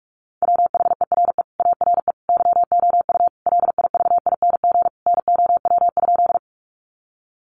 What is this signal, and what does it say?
Morse code, heard as a keyed tone, about 44 wpm, 700 Hz. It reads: WHERE ARE YOU LIVING NOW?
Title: Sentences from Top 1100 Words 45wpm